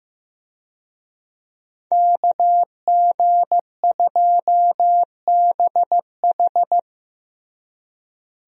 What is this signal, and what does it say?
Morse code, heard as a keyed tone, about 15 wpm, 700 Hz. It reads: KG2BH
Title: USA Call Signs - 5 Characters - Repeated 3x - 15wpm